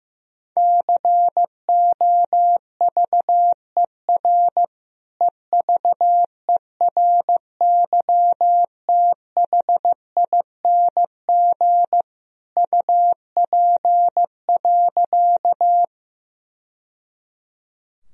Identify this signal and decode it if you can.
Morse code, heard as a keyed tone, about 15 wpm, 700 Hz. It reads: COVER EVERYTHING UP.
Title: Sentences from Top 1000 Words - Review Part 1 - 15wpm